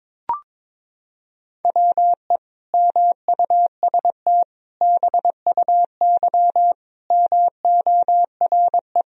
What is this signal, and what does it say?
Morse code, heard as a keyed tone, about 22 wpm, 700 Hz. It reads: WE MUST BUY MORE
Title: Sentences from Top 1000 Words - Review Part 4 - 22wpm